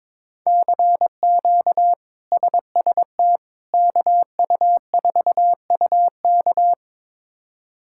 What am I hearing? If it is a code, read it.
Morse code, heard as a keyed tone, about 22 wpm, 700 Hz. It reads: CQ SST KU4UK
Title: K1USN SST QSO Practice - 22wpm